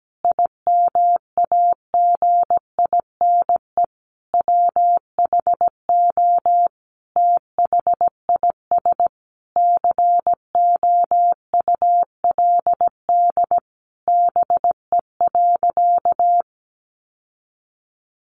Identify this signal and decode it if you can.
Morse code, heard as a keyed tone, about 17 wpm, 700 Hz. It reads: IMAGINE WHO THIS COULD BE.
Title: Sentences from Top 1400 Words 17wpm